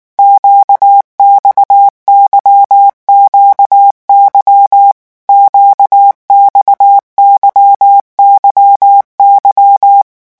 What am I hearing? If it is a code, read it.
Morse code, heard as a keyed tone, about 19 wpm, 800 Hz. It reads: QXYQY QXYYY
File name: revision1-vitesse 18 mots